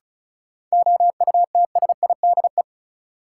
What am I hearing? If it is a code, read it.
Morse code, heard as a keyed tone, about 34 wpm, 700 Hz. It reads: OUTSIDE
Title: Sentences from Top 900 Words - New Word Drill - 35wpm